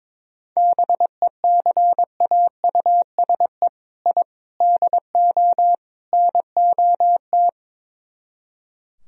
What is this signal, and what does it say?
Morse code, heard as a keyed tone, about 22 wpm, 700 Hz. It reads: BECAUSE I DO NOT
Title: Speed-Racing - Sets of 4 Words 22wpm